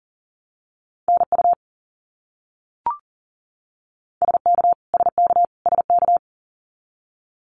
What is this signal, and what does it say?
Morse code, heard as a keyed tone, about 40 wpm, 700 Hz. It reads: NU SK SK SK
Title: Canadian Provinces and Territories - Repeated 3x - 40wpm